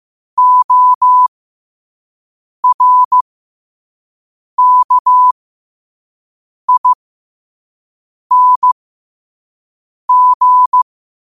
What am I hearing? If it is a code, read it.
Morse code, heard as a keyed tone, about 15 wpm, spaced as for 5 wpm, 1000 Hz. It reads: ORKING